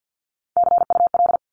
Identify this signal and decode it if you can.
Morse code, heard as a keyed tone, about 50 wpm, 700 Hz. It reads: CUL
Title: Abbreviations Repeated 3x - 50wpm